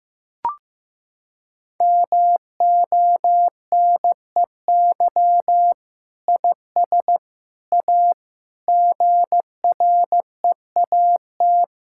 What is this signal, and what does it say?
Morse code, heard as a keyed tone, about 15 wpm, 700 Hz. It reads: MONEY IS A GREAT
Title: Sentences from Top 1600 Words 15wpm